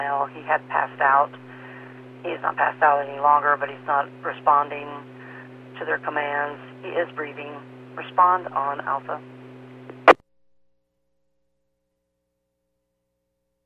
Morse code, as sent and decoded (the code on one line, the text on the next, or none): none